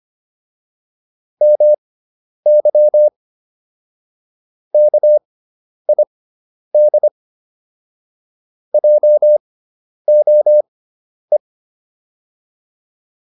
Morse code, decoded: MY KID JOE

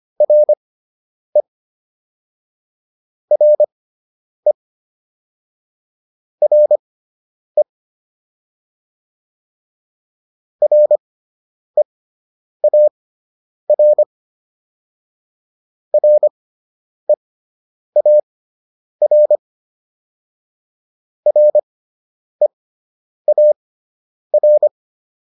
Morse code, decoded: RE RE RE REAR REAR REAR

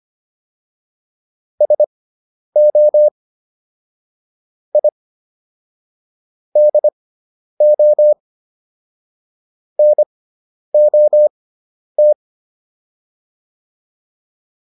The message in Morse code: SO I DO NOT

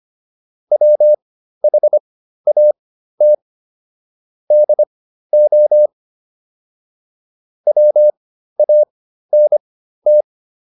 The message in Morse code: WHAT DO WANT

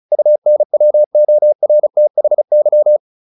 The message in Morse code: UNWORTHY

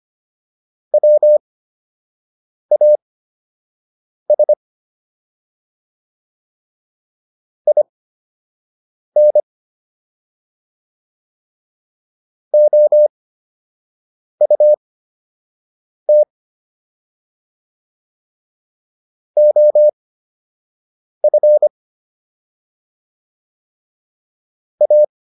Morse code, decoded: WAS IN OUT OF A

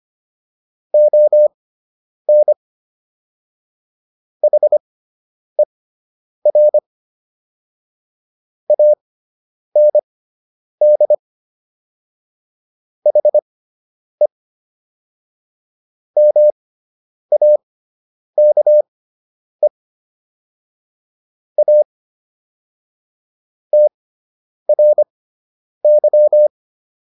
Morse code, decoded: ON HER AND HE MAKE A TRY

